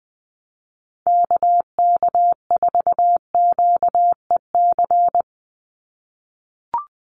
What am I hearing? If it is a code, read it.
Morse code, heard as a keyed tone, about 20 wpm, 700 Hz. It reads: KK4QEC
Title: USA Call Signs 20wpm